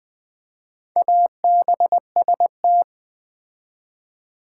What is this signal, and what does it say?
Morse code, heard as a keyed tone, about 20 wpm, 700 Hz. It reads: ABST
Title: Build A Word - 4 Letters At A Time - 8-Letter Words - 20wpm